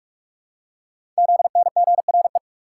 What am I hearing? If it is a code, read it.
Morse code, heard as a keyed tone, about 44 wpm, 690 Hz. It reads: ZNGRE
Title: Code Groups - 5 Letters - 45wpm